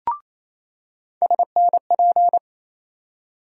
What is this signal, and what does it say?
Morse code, heard as a keyed tone, about 28 wpm, 700 Hz. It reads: SNP
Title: USA Call Sign Postfixes - 3 Letters 28wpm